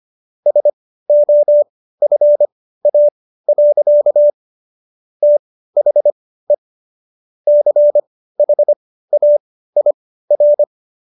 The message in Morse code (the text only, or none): SOFA. THE CHAIR